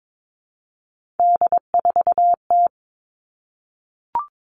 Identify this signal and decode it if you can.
Morse code, heard as a keyed tone, about 22 wpm, 700 Hz. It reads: D4T
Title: All Cut Numbers - 3-Digit 22wpm